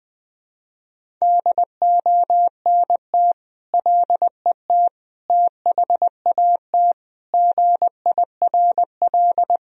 Morse code, heard as a keyed tone, about 21 wpm, 700 Hz. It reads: DONT LET THAT GIRL
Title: Sentences from Top 1000 Words - Review Part 1 - 20wpm